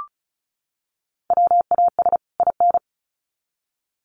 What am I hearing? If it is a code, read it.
Morse code, heard as a keyed tone, about 34 wpm, 700 Hz. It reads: WAS IN
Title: Sets of 2 Words 35wpm